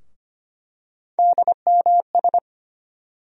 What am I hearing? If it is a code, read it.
Morse code, heard as a keyed tone, about 25 wpm, 700 Hz. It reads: DMS
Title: USA Call Sign Postfixes - 3 Letters 25wpm